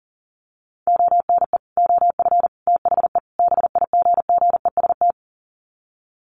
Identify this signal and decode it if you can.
Morse code, heard as a keyed tone, about 40 wpm, 700 Hz. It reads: ONE OF THE BIGGEST
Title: Sets of 4 Words 40wpm